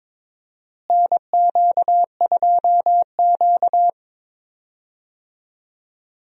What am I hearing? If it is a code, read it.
Morse code, heard as a keyed tone, about 22 wpm, 700 Hz. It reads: NQ2Q
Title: USA Call Signs - 4 Characters - Repeated 3x - 22wpm